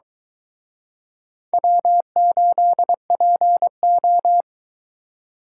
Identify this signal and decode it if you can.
Morse code, heard as a keyed tone, about 23 wpm, 700 Hz. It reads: W8PO